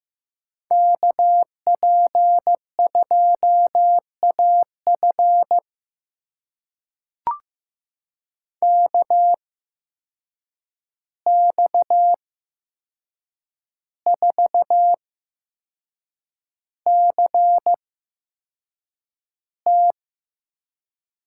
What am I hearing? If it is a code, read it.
Morse code, heard as a keyed tone, about 15 wpm, spaced as for 6 wpm, 700 Hz. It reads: KP2AF KX4CT
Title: USA Call Signs - 5 Characters - 8x Character Spacing - 15wpm